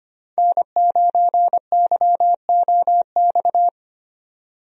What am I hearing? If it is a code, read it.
Morse code, heard as a keyed tone, about 25 wpm, 690 Hz. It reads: N9YOX
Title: USA Call Signs - 5 Characters - 25wpm